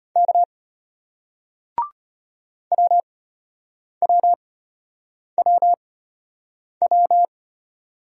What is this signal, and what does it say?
Morse code, heard as a keyed tone, about 34 wpm, spaced as for 8 wpm, 700 Hz. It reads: K WWWW